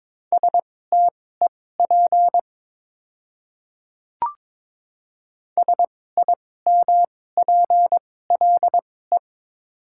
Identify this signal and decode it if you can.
Morse code, heard as a keyed tone, about 21 wpm, spaced as for 16 wpm, 700 Hz. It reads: STEP SIMPLE
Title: 2x Character Spacing - Top 500 Words - 22wpm